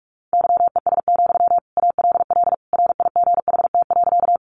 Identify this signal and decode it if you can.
Morse code, heard as a keyed tone, about 44 wpm, 700 Hz. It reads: YES, ALL RIGHT.
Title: Sentences from Top 500 Words 45wpm